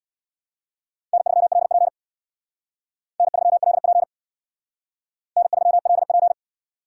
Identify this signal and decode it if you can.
Morse code, heard as a keyed tone, about 62 wpm, 700 Hz. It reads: N3LP N3LP N3LP
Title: Speed Racing - USA Call Signs 50wpm